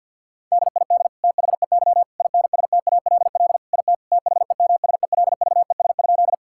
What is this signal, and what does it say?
Morse code, heard as a keyed tone, about 50 wpm, 700 Hz. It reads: DID THEY INSTALL IT THEMSELVES?
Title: Sentences from Top 1600 Words 50wpm